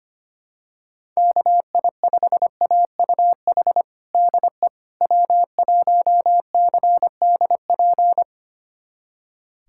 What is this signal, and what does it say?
Morse code, heard as a keyed tone, about 25 wpm, 700 Hz. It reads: KI5AUH DE W1CDP
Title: USA Callsign DE Callsign 25wpm